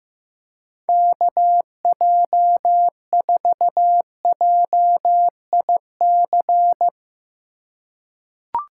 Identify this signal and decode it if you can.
Morse code, heard as a keyed tone, about 15 wpm, 700 Hz. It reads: KJ4JIC
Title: USA Call Signs - 6 Characters - 4x Character Spacing - 15wpm